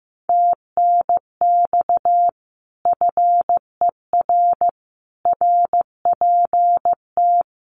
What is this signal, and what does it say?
Morse code, heard as a keyed tone, about 15 wpm, 700 Hz. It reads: TNX FER RPT
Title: Common QSO Phrases Encore 15wpm